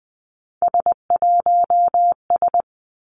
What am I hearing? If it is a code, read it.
Morse code, heard as a keyed tone, about 19 wpm, 700 Hz. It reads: S1S
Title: Three Challenging Letter-Number Combinations 20wpm